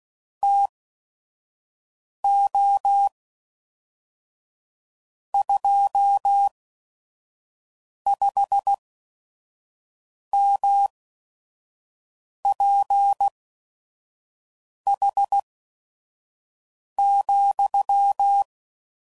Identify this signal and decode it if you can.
Morse code, heard as a keyed tone, about 16 wpm, spaced as for 5 wpm, 800 Hz. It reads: TO 25MPH,
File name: CW Code Groups @5WPM / Weather Sets / Weather 5 5